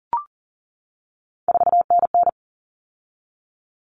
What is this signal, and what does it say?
Morse code, heard as a keyed tone, about 40 wpm, 700 Hz. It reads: 4NN
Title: All Cut Numbers - 3-Digit 40wpm